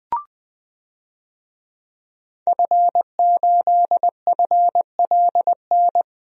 Morse code, heard as a keyed tone, about 20 wpm, 700 Hz. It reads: F8FLN
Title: DXCC-100 Call Signs - 5 Characters - Repeated 2x - 20wpm